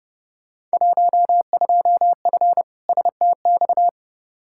Morse code, heard as a keyed tone, about 29 wpm, 700 Hz. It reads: 12F STX
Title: Field Day Exchanges Encore - 30wpm